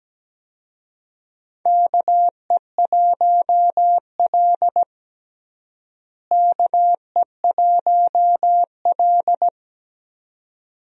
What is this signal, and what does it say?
Morse code, heard as a keyed tone, about 17 wpm, 700 Hz. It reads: KE1L KE1L
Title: USA Call Signs - 4 Characters - Repeated 2x - 17wpm